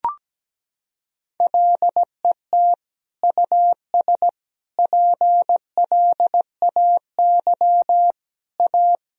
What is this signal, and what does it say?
Morse code, heard as a keyed tone, about 17 wpm, 700 Hz. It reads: LET US PLAY A
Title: Sentences from Top 1000 Words - Review Part 3 - 17wpm